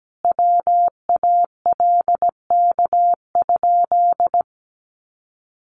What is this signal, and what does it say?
Morse code, heard as a keyed tone, about 17 wpm, 700 Hz. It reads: WALK?
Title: Sentences from Top 1100 Words 17wpm